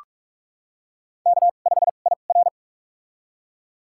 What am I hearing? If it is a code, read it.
Morse code, heard as a keyed tone, about 45 wpm, spaced as for 32 wpm, 700 Hz. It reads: K5IR
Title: USA Call Signs - 4 Characters - 2x Character Spacing - 45wpm